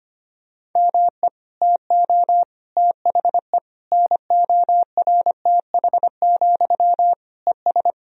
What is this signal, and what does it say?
Morse code, heard as a keyed tone, about 25 wpm, 700 Hz. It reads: ME TO THE NORTH, ES
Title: Sentences from Top 400 Words and Abbreviations 25wpm